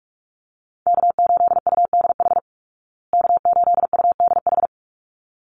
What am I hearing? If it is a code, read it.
Morse code, heard as a keyed tone, about 44 wpm, 700 Hz. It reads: K8UDH K8UDH
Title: Call Signs Repeated 3x - 45wpm